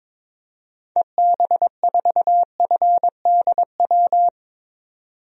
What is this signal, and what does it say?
Morse code, heard as a keyed tone, about 22 wpm, 700 Hz. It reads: EB4FDW
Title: DXCC-100 Call Signs - 6 Characters - 22wpm